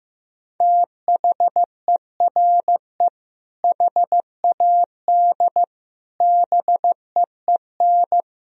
Morse code, heard as a keyed tone, about 15 wpm, 700 Hz. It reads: THERE HAD BEEN
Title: Sentences from Top 1000 Words - Review Part 2 - 15wpm